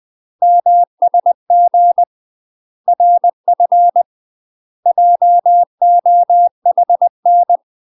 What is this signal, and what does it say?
Morse code, heard as a keyed tone, about 20 wpm, 700 Hz. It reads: MSG RF JOHN
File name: UK MASTER QSOs TRAINING / 20WPM / UK MASTER QSOs 0024